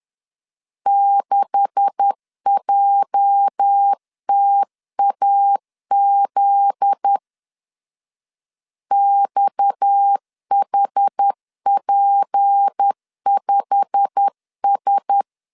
Morse code, heard as a keyed tone, about 10 wpm, 800 Hz. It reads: 6JTAZ XHP5S
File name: LEZIONI / 10 SIGNS / 10A SLASH 10wpm 2808